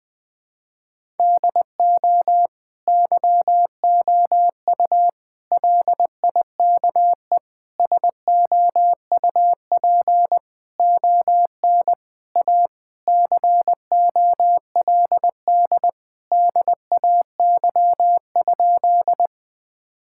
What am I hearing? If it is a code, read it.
Morse code, heard as a keyed tone, about 20 wpm, 700 Hz. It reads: DO YOU LIKE SOUP ON A COLD DAY?
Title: Sentences from Top 1000 Words - Review Part 6 - 20wpm